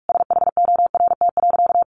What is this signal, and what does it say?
Morse code, heard as a keyed tone, about 44 wpm, 700 Hz. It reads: SHORT.
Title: Sentences from Top 500 Words 45wpm